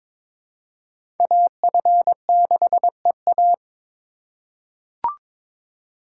Morse code, heard as a keyed tone, about 22 wpm, 700 Hz. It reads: AF6EA
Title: USA Call Signs - 2x Character Spacing - 22wpm